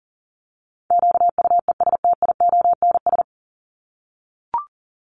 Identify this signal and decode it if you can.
Morse code, heard as a keyed tone, about 40 wpm, 700 Hz. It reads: QUESTIONS
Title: ES:S Suffix - Top 100 Words with Base Words 40wpm